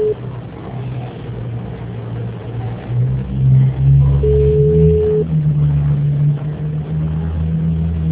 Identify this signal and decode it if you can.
Ringback tone (Europe, 425 Hz).